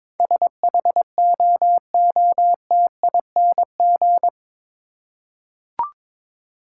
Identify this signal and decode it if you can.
Morse code, heard as a keyed tone, about 22 wpm, 700 Hz. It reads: SHOOTING